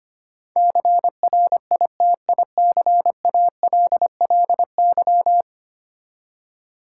Morse code, heard as a keyed tone, about 25 wpm, 700 Hz. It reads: CRITICALLY
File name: Ten Letter Words - 25wpm